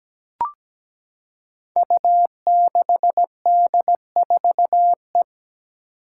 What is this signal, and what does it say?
Morse code, heard as a keyed tone, about 17 wpm, 700 Hz. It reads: U6D4E
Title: Five Challenging Letter-Number Combinations 17wpm